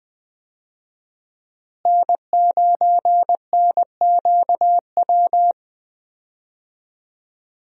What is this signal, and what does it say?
Morse code, heard as a keyed tone, about 20 wpm, 700 Hz. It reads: N9NQW
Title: USA Call Signs - 5 Characters - Repeated 3x - 20wpm